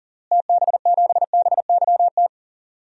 Morse code, heard as a keyed tone, about 40 wpm, 700 Hz. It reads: TB7BYT
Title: DXCC-100 Call Signs - 6 Characters - Repeated 3x - 40wpm